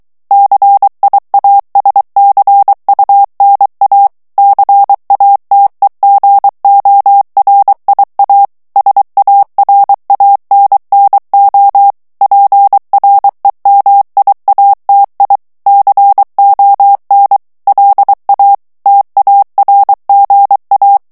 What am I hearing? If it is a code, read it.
Morse code, heard as a keyed tone, about 23 wpm, 800 Hz. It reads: CIASCUNA CATEGORIA SARANNO PREMIATI CON LA TARGA
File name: ITALIEN texte clair 2 vitesse 22 mots minute